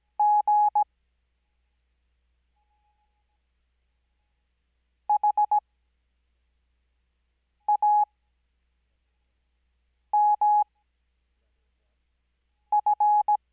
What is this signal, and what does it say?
Morse code, heard as a keyed tone, about 17 wpm, spaced as for 3 wpm, 840 Hz. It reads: GHAMF
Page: Farnsworth Spacing: 15WPM Character speed Spaced at 5WPM.